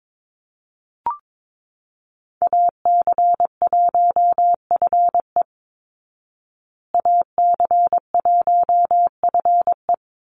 Morse code, heard as a keyed tone, about 22 wpm, 700 Hz. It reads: AC1FE AC1FE
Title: Call Signs Repeated 3x - 22wpm